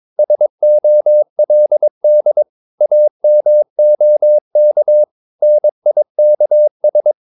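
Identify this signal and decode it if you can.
Morse code, heard as a keyed tone, about 22 wpm, 600 Hz. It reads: SOLD AMOK NIKS
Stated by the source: Morse practice of short Danish words at 22 wpm, from CW med Gnister.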